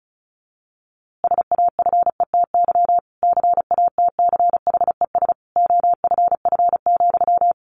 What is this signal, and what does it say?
Morse code, heard as a keyed tone, about 34 wpm, 700 Hz. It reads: SAFETY CATCHES OFF,
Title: A Sound Of Thunder 35wpm